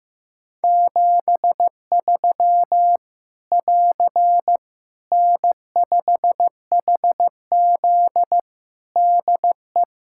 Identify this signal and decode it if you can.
Morse code, heard as a keyed tone, about 15 wpm, 700 Hz. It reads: 73 <AR> N5HZ DE
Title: Practice QSO - Exchange 4 - Ending - 15wpm